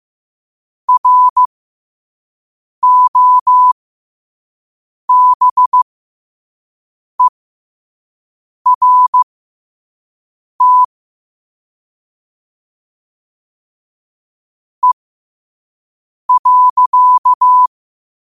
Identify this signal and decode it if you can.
Morse code, heard as a keyed tone, about 15 wpm, spaced as for 5 wpm, 1000 Hz. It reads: ROBERT E.
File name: qotdfiles / 5wpm